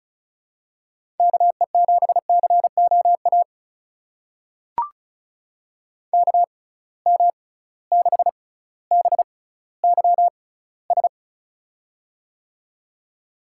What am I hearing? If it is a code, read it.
Morse code, heard as a keyed tone, about 34 wpm, spaced as for 18 wpm, 700 Hz. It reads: KE7COA KM6BYS
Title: USA Call Signs - 6 Characters - 6x Character Spacing - 35wpm